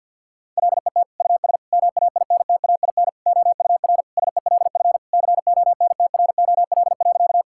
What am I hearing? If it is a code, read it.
Morse code, heard as a keyed tone, about 50 wpm, 700 Hz. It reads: LET US MAINTAIN OUR SELF CONTROL.
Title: Sentences from Top 1000 Words - Review Part 5 - 50wpm